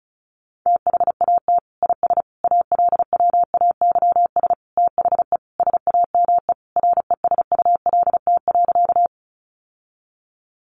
Morse code, heard as a keyed tone, about 34 wpm, 700 Hz. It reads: THAT IS ALWAYS THE SAME RESULT.
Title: Sentences from Top 700 Words 35wpm